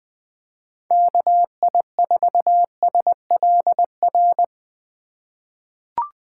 Morse code, heard as a keyed tone, about 20 wpm, 700 Hz. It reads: KI4SLR
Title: USA Call Signs - 6 Characters - 2x Character Spacing - 20wpm